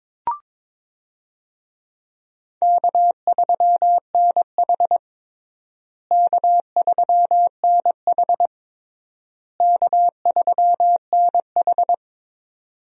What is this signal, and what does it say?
Morse code, heard as a keyed tone, about 22 wpm, 700 Hz. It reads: K3NH K3NH K3NH
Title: USA Call Signs - 4 Characters - Repeated 3x - 22wpm